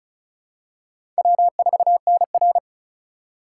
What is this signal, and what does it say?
Morse code, heard as a keyed tone, about 34 wpm, 700 Hz. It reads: W4NR
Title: USA Call Signs - 2x Character Spacing - 35wpm